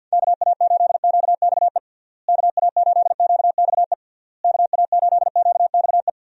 Morse code, heard as a keyed tone, about 47 wpm, 700 Hz. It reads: KA8QXE KA8QXE KA8QXE
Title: USA Call Signs - 6 Characters - Repeated 3x - 50wpm